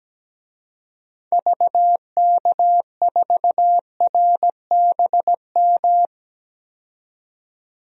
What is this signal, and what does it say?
Morse code, heard as a keyed tone, about 17 wpm, 700 Hz. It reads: VK4RBM
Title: DXCC-100 Call Signs - 6 Characters - 17wpm